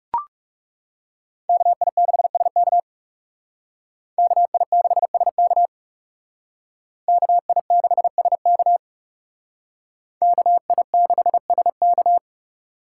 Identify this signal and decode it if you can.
Morse code, heard as a keyed tone, about 38 wpm, 700 Hz. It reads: KI6SK KI6SK KI6SK KI6SK